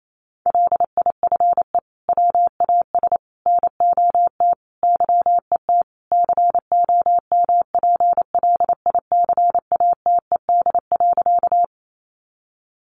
Morse code, heard as a keyed tone, about 28 wpm, 700 Hz. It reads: LIFE WAS NOT YET COMPLICATED.
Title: Sentences from Top 1200 Words 28wpm